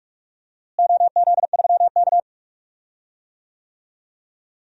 Morse code, heard as a keyed tone, about 43 wpm, 700 Hz. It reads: OZ3K
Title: DXCC-100 Call Signs - 4 Characters - Repeated 3x - 45wpm